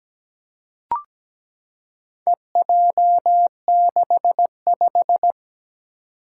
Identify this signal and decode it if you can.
Morse code, heard as a keyed tone, about 17 wpm, 700 Hz. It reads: EJ65
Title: Four Challenging Letter-Number Combinations 17wpm